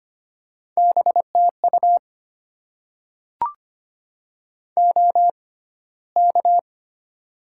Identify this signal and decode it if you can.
Morse code, heard as a keyed tone, about 25 wpm, spaced as for 15 wpm, 700 Hz. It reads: BTU OK
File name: Top 90 QSO Elements - 6x Character Spacing - 25wpm